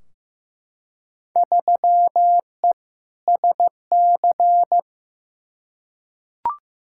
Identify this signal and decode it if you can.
Morse code, heard as a keyed tone, about 15 wpm, 700 Hz. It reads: 3E SC
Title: Field Day Exchanges Encore - 15wpm